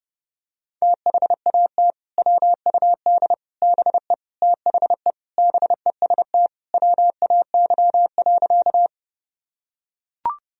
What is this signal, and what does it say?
Morse code, heard as a keyed tone, about 32 wpm, 700 Hz. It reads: THAT WUD BE THE BEST WAY.
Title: Sentences from Top 400 Words and Abbreviations 30wpm